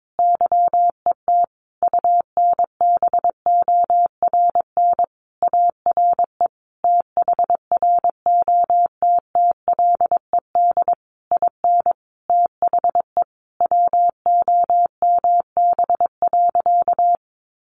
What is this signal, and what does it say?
Morse code, heard as a keyed tone, about 22 wpm, 700 Hz. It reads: YET UNBORN ARE THROTTLED IN THE WOMB.
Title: A Sound Of Thunder 22wpm